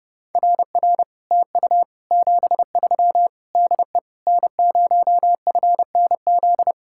Morse code, heard as a keyed tone, about 34 wpm, 700 Hz. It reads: RR TU 73 DE N0FNZ
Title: Speed-Racing Common QSO Phrases - 30wpm